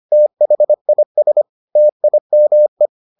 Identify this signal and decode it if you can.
Morse code, heard as a keyed tone, about 25 wpm, 600 Hz. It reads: THIS TIME